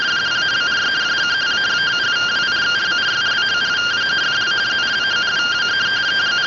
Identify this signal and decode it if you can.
Two-tone teleprinter signal, about 170 Hz shift.
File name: findings / SAGITAL nb203 fr61 3mm.gen-17 rtty rx